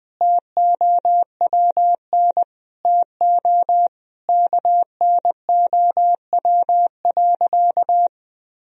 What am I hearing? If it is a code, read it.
Morse code, heard as a keyed tone, about 20 wpm, 700 Hz. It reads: TOWN TO KNOW.
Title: Sentences from Top 1000 Words - Review Part 2 - 20wpm